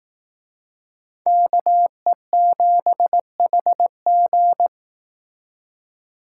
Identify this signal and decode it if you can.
Morse code, heard as a keyed tone, about 18 wpm, 700 Hz. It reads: KE7HG
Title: Speed Racing - USA 2x2 Call Signs 15wpm